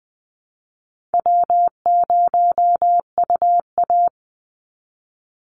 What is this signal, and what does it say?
Morse code, heard as a keyed tone, about 20 wpm, 700 Hz. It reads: W0UA
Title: Easy Call Signs Repeated 3x - 20wpm